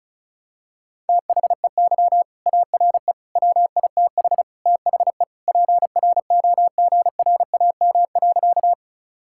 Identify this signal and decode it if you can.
Morse code, heard as a keyed tone, about 36 wpm, 700 Hz. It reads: THEY ARE WITH THE PROGRAM.
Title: Sentences from Top 1000 Words - Review Part 5 - 35wpm